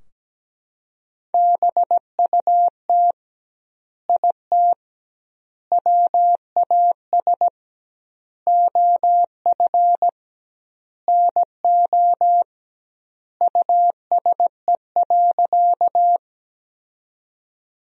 Morse code, heard as a keyed tone, about 17 wpm, 700 Hz. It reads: BUT IT WAS OF NO USE.
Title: 2x Word-Spacing - Sentences from Top 100 Words - 17wpm